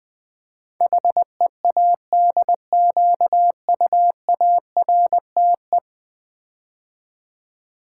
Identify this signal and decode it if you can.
Morse code, heard as a keyed tone, about 20 wpm, 700 Hz. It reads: HEADQUARTE
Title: Build A Word - 5 Letters At A Time - 12-Letter Words - 20wpm